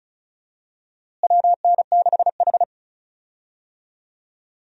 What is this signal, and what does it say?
Morse code, heard as a keyed tone, about 36 wpm, 700 Hz. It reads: WN6H
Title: USA Call Signs - 4 Characters - Repeated 2x - 35wpm